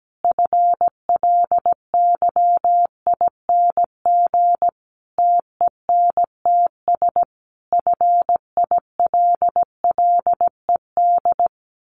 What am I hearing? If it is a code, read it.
Morse code, heard as a keyed tone, about 17 wpm, 700 Hz. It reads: FLYING TENTS FILLED
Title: A Sound Of Thunder 17wpm